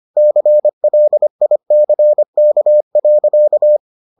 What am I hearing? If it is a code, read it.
Morse code, heard as a keyed tone, about 25 wpm, 600 Hz. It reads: CLICK.